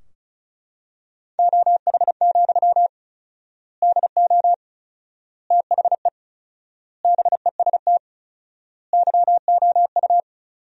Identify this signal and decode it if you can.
Morse code, heard as a keyed tone, about 36 wpm, 700 Hz. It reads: OH, DO THE BEST YOU
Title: 4x Word-Spacing - Sentences from Top 500 Words - 35wpm